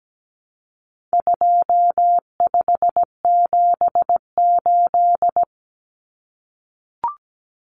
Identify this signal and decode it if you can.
Morse code, heard as a keyed tone, about 17 wpm, 700 Hz. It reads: 2578
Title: Common Cut Numbers - 4-Digit 17wpm